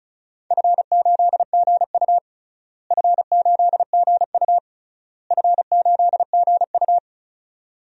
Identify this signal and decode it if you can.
Morse code, heard as a keyed tone, about 34 wpm, 700 Hz. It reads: F8GU F8GU F8GU
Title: DXCC-100 Call Signs - 4 Characters - Repeated 3x - 35wpm